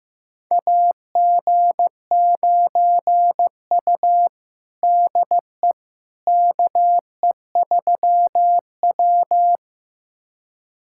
Morse code, heard as a keyed tone, about 15 wpm, 700 Hz. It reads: AG9U DE KE3W
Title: USA Callsign DE Callsign - 4 Characters - 15wpm